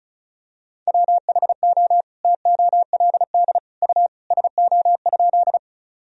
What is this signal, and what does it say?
Morse code, heard as a keyed tone, about 36 wpm, 700 Hz. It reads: WHO TOLD U SO?
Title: Sentences from Top 400 Words and Abbreviations 35wpm